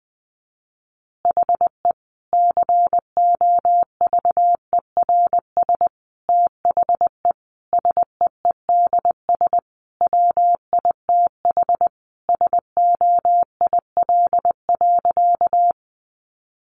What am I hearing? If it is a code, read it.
Morse code, heard as a keyed tone, about 20 wpm, 700 Hz. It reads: HE COVERS THE SEEDS WITH SOIL.
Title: Easy Phrases 20wpm